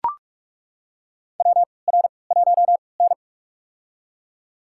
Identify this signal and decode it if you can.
Morse code, heard as a keyed tone, about 41 wpm, spaced as for 25 wpm, 700 Hz. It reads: WR1N
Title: USA Call Signs - 4 Characters - 3x Character Spacing - 45wpm